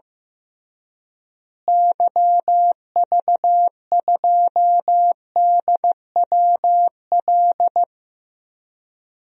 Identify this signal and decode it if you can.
Morse code, heard as a keyed tone, about 15 wpm, 700 Hz. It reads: YV2DWL
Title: DXCC-100 Call Signs - 6 Characters - Repeated 3x - 15wpm